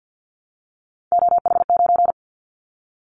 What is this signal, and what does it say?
Morse code, heard as a keyed tone, about 50 wpm, 700 Hz. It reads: OH9
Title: Callsign Prefixes for Top 25 DXCC - Repeated 3x - 50wpm